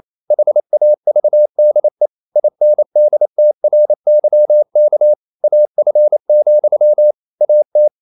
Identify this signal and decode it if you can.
Morse code, heard as a keyed tone, about 28 wpm, 600 Hz. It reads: HAVDE INDTRYK AF, AT
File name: Ferie QRQ 28wpm